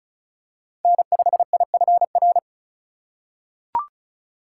Morse code, heard as a keyed tone, about 34 wpm, 700 Hz. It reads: N5IFR
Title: USA Call Signs - 5 Characters - 3x Character Spacing - 35wpm